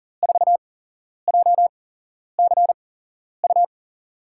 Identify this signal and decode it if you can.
Morse code, heard as a keyed tone, about 39 wpm, spaced as for 11 wpm, 700 Hz. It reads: 4JCU
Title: USA Call Signs - 5 Characters - 8x Character Spacing - 40wpm